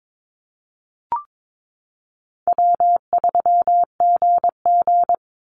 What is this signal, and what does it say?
Morse code, heard as a keyed tone, about 22 wpm, 700 Hz. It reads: W3GG
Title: Easy Call Signs Repeated 3x - 22wpm